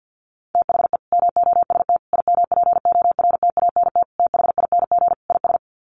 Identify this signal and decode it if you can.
Morse code, heard as a keyed tone, about 50 wpm, 700 Hz. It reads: THE MOST IMPORTANT THING IS
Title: Sets of 5 Words Encore 50wpm